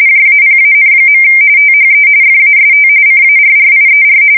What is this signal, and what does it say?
RTTY, about 45.45 baud. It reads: JAPAN. THE FAST CARRIER
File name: fsk170 / USS-Massachusetts